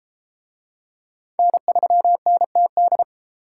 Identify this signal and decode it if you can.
Morse code, heard as a keyed tone, about 28 wpm, 700 Hz. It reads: N3NTD